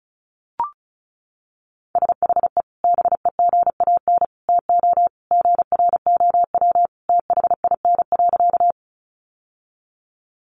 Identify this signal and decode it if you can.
Morse code, heard as a keyed tone, about 34 wpm, 700 Hz. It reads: SHE BEGAN TO GROW THIN.
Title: Sentences from Top 1200 Words 35wpm